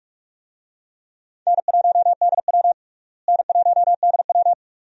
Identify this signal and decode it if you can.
Morse code, heard as a keyed tone, about 42 wpm, 700 Hz. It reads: N1DW N1DW
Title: USA Call Signs - 4 Characters - Repeated 2x - 45wpm